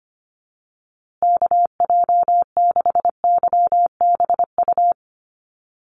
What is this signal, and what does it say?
Morse code, heard as a keyed tone, about 25 wpm, 700 Hz. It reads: KJ6YBU
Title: USA Call Signs Repeated 2x 25wpm